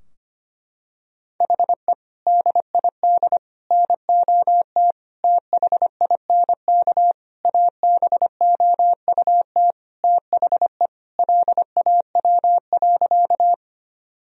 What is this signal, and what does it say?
Morse code, heard as a keyed tone, about 25 wpm, 700 Hz. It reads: HE DID NOT THINK ABOUT THE LAW.
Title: Sentences from Top 1000 Words - Review Part 3 - 25wpm